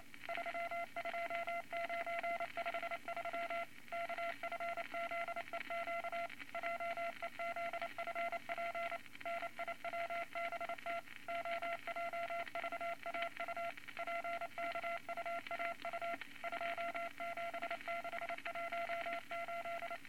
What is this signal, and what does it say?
Morse code, heard as a keyed tone, about 28 wpm, 680 Hz, with noise. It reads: 32953 KFZEQ JEZFP NIWBT OJVAU PKUAU 27618
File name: Morse Code Records / 22 A LCM 172 Twenty Words Per Minute